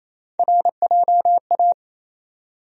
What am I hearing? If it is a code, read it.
Morse code, heard as a keyed tone, about 28 wpm, 700 Hz. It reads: RJA
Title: USA Call Sign Postfixes - 3 Letters 28wpm